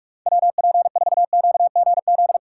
Morse code, heard as a keyed tone, about 36 wpm, 700 Hz. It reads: WP4QGZ
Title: Speed Racing - USA Call Signs 30wpm